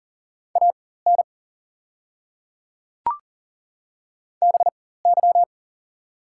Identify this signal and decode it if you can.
Morse code, heard as a keyed tone, about 40 wpm, spaced as for 19 wpm, 690 Hz. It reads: AN BY
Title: 4x Character Spacing - Top 300 Words - 40wpm